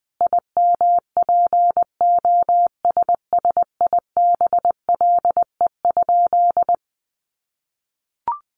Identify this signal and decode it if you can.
Morse code, heard as a keyed tone, about 20 wpm, 700 Hz. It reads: IMPOSSIBLE?
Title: Sentences from Top 1100 Words 20wpm